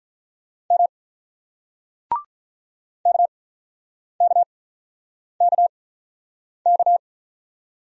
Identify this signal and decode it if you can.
Morse code, heard as a keyed tone, about 44 wpm, 700 Hz. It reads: M K K K K